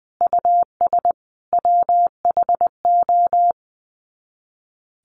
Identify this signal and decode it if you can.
Morse code, heard as a keyed tone, about 20 wpm, 700 Hz. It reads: US WHO
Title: Sets of 5 Words Encore 20wpm